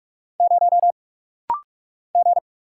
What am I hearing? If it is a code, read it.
Morse code, heard as a keyed tone, about 44 wpm, 700 Hz. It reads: 0 G
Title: Single Letter-Number - Rapid-Fire 45wpm